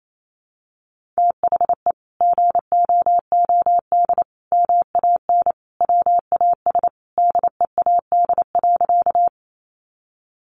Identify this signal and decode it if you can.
Morse code, heard as a keyed tone, about 28 wpm, 700 Hz. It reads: THE GOOD MAN WAS DEAD.
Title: Sentences from Top 1500 Words 28wpm